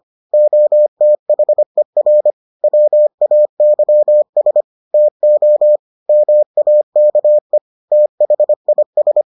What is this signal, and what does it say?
Morse code, heard as a keyed tone, about 25 wpm, 600 Hz. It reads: OTHER WAYS TO MAKE THIS